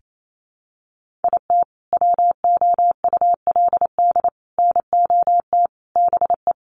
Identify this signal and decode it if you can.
Morse code, heard as a keyed tone, about 28 wpm, 700 Hz. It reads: IT WOULD NOT BE
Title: Sentences from Top 1200 Words 28wpm